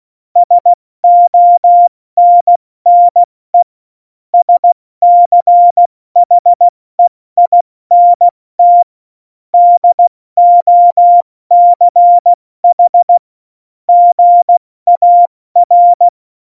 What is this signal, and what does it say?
Morse code, heard as a keyed tone, about 16 wpm, 700 Hz. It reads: SONNE SCHEINT DOCH GAR
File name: Morsekurs / Witze / Witz 22. Mai 2018, 70 F80